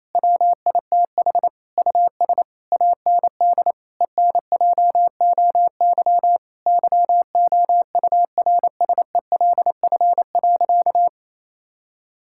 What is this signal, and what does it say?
Morse code, heard as a keyed tone, about 28 wpm, 700 Hz. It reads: WITH US AND ENJOY YOURSELF.
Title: Sentences from Top 1000 Words - Review Part 5 - 28wpm